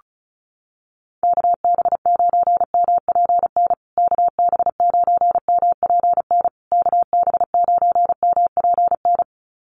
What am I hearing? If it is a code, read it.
Morse code, heard as a keyed tone, about 34 wpm, 700 Hz. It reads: KB9MPN KB9MPN KB9MPN
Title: USA Call Signs Repeated 3x 35wpm